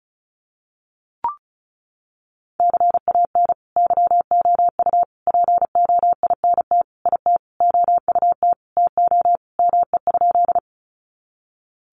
Morse code, heard as a keyed tone, about 34 wpm, 700 Hz. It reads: CAN YOU POINT IT OUT TO ME?
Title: Sentences from Top 200 Words 35wpm